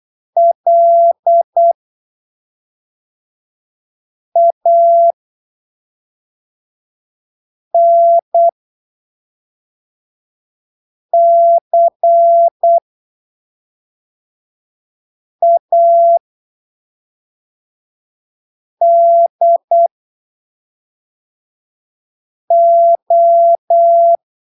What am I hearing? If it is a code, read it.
Morse code, heard as a keyed tone, about 8 wpm, spaced as for 3 wpm, 680 Hz. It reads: LANCADO
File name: arquivos / texto / texto 16 08ppm